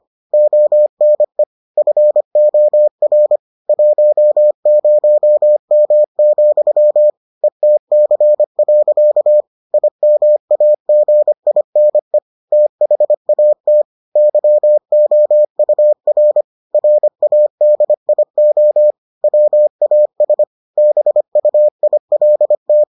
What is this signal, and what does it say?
Morse code, heard as a keyed tone, about 25 wpm, 600 Hz. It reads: ONE FOR 10M, ETC. IMAGINE THAT YOUR RADIO WAS BUILT